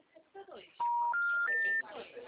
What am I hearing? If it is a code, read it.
Special information tone, with speech.
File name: out- -2007- - - .45186